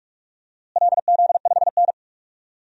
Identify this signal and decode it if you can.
Morse code, heard as a keyed tone, about 46 wpm, 700 Hz. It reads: LZ5N
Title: DXCC-100 Call Signs - 4 Characters - Repeated 2x - 45wpm